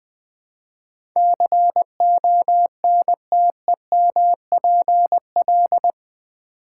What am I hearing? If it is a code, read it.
Morse code, heard as a keyed tone, about 20 wpm, 700 Hz. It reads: CONTEMPL
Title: Build A Word - 4 Letters At A Time - 11-Letter Words - 20wpm